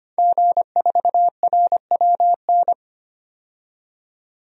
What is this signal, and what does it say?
Morse code, heard as a keyed tone, about 25 wpm, 700 Hz. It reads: G4RWN
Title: DXCC-100 Call Signs - 5 Characters - 25wpm